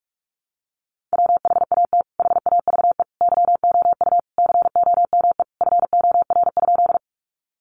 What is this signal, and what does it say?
Morse code, heard as a keyed tone, about 44 wpm, 700 Hz. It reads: WHAT HAVE YOU COME FOR?
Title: Sentences from Top 100 Words 45wpm